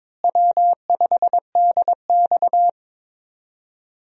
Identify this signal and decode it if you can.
Morse code, heard as a keyed tone, about 22 wpm, 700 Hz. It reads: W5DX
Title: USA Call Signs - 4 Characters - Repeated 2x - 22wpm